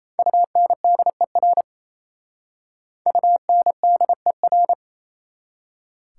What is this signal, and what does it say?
Morse code, heard as a keyed tone, about 36 wpm, 690 Hz. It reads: UNDER UNDER